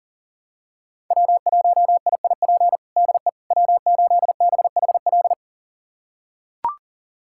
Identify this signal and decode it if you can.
Morse code, heard as a keyed tone, about 40 wpm, 700 Hz. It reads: W1IIP DE W8BHL
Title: USA Callsign DE Callsign - 5 Characters - 40wpm